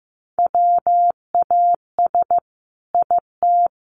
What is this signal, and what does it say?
Morse code, heard as a keyed tone, about 14 wpm, 700 Hz. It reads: WAS IT
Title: Sentences from Top 1200 Words 15wpm